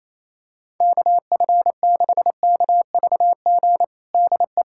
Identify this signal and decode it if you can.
Morse code, heard as a keyed tone, about 28 wpm, 700 Hz. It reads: KF6KVG DE
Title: USA Callsign DE Callsign 28wpm